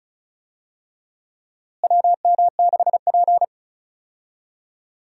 Morse code, heard as a keyed tone, about 34 wpm, 700 Hz. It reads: WM6P
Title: USA Call Signs - 4 Characters - Repeated 3x - 35wpm